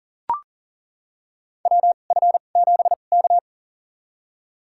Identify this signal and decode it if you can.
Morse code, heard as a keyed tone, about 40 wpm, spaced as for 29 wpm, 700 Hz. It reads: WF7K
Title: USA Call Signs - 4 Characters - 2x Character Spacing - 40wpm